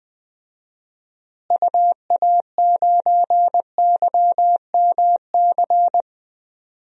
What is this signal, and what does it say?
Morse code, heard as a keyed tone, about 20 wpm, 700 Hz. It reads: UA9YMC
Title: DXCC-100 Call Signs - 6 Characters - Repeated 2x - 20wpm